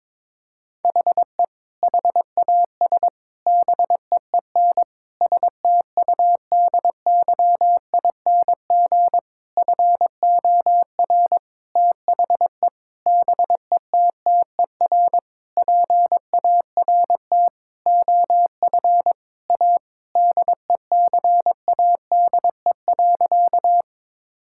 Morse code, heard as a keyed tone, about 22 wpm, 700 Hz. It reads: HE HAS BEEN STUDYING FOR THE BETTER PART OF A DECADE.
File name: Sentences with 7-Word Phrases - 22wpm